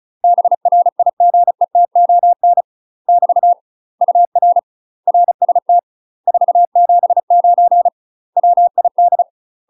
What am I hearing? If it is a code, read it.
Morse code, heard as a keyed tone, about 34 wpm, 700 Hz. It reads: BRIGETON <BT> UR RST 479 WID
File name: UK MASTER QSOs TRAINING / 35WPM / UK MASTER QSOs 0024